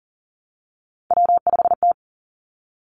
Morse code, heard as a keyed tone, about 42 wpm, 700 Hz. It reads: W5T
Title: Three Challenging Letter-Number Combinations 40wpm